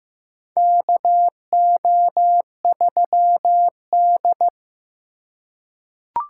Morse code, heard as a keyed tone, about 15 wpm, 700 Hz. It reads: KO3D
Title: USA Call Signs - 4 Characters - 3x Character Spacing - 15wpm